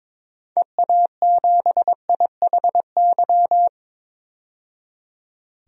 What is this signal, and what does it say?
Morse code, heard as a keyed tone, about 22 wpm, 700 Hz. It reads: EA7IHY
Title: DXCC-100 Call Signs - 6 Characters - 22wpm